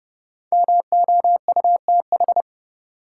Morse code, heard as a keyed tone, about 31 wpm, 700 Hz. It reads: MOUTH
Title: Sentences from Top 1500 Words - New Word Drill - 30wpm